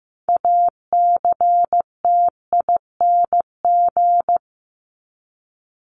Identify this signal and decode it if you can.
Morse code, heard as a keyed tone, about 15 wpm, 700 Hz. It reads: ACTING
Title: ING Suffix - Top 100 Words with Base Words 15wpm